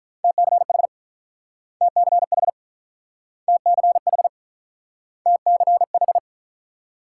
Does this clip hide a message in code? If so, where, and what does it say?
Yes: Morse code from 0.2 to 6.2 s, heard as a keyed tone, about 49 wpm, 700 Hz. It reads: TCH TCH TCH TCH